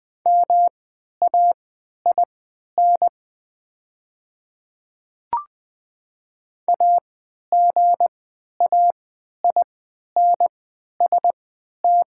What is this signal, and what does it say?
Morse code, heard as a keyed tone, about 20 wpm, spaced as for 11 wpm, 700 Hz. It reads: MAIN AGAINST
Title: 3x Character Spacing - Top 500 Words - 20wpm